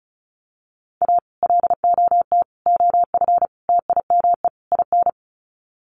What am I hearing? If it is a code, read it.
Morse code, heard as a keyed tone, about 34 wpm, 700 Hz. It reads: A LOT OF TIME IN
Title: Sets of 5 Words Encore2 35wpm